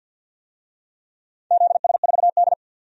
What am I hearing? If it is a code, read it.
Morse code, heard as a keyed tone, about 53 wpm, 700 Hz. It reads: ZS4D
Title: DXCC-100 Call Signs - 4 Characters - 50wpm